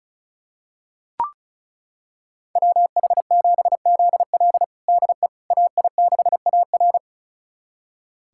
Morse code, heard as a keyed tone, about 36 wpm, 700 Hz. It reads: WH7ZL DE AI6AR
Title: USA Callsign DE Callsign - 5 Characters - 35wpm